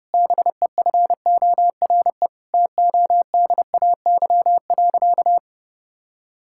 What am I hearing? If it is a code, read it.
Morse code, heard as a keyed tone, about 30 wpm, 700 Hz. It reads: BEFORE TODAY.
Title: Sentences from Top 1000 Words - Review Part 6 - 30wpm